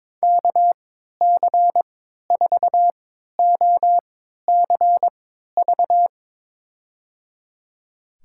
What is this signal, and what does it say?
Morse code, heard as a keyed tone, about 22 wpm, spaced as for 13 wpm, 700 Hz. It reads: KC4OCV
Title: USA Call Signs - 6 Characters - 3x Character Spacing - 22wpm